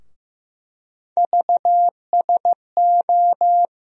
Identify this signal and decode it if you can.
Morse code, heard as a keyed tone, about 15 wpm, 700 Hz. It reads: VSO
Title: USA Call Sign Postfixes - 3 Letters 15wpm